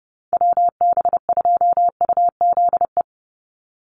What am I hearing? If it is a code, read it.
Morse code, heard as a keyed tone, about 30 wpm, 700 Hz. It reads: WB2UZE
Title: Call Signs - 30wpm